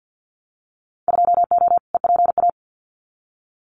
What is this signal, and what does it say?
Morse code, heard as a keyed tone, about 50 wpm, 700 Hz. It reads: 2O EPA
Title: Winter Field Day Exchanges Encore 50wpm